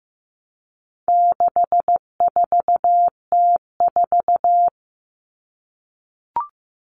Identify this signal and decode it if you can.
Morse code, heard as a keyed tone, about 15 wpm, 700 Hz. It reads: 64T4
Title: All Cut Numbers - 4-Digit 15wpm